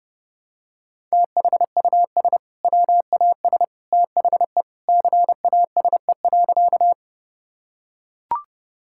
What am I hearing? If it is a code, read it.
Morse code, heard as a keyed tone, about 30 wpm, 700 Hz. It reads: THUS WAS THE CASE.
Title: Sentences from Top 1000 Words - Review Part 5 - 30wpm